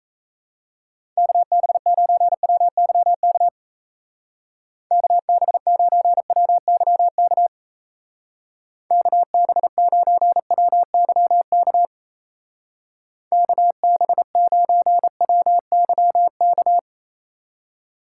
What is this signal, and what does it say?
Morse code, heard as a keyed tone, about 36 wpm, 700 Hz. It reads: KB9WYK KB9WYK KB9WYK KB9WYK